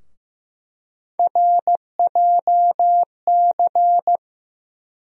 Morse code, heard as a keyed tone, about 15 wpm, 700 Hz. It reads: RJC
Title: USA Call Sign Postfixes - 3 Letters 15wpm